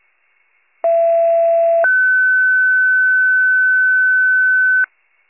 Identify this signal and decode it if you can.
Two-tone page, heard as a one-second tone, then a three-second tone, about 670 then 1500 Hz. It is heard from 0.8 to 4.8 s.
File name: pager / 39 FIRE / wcalert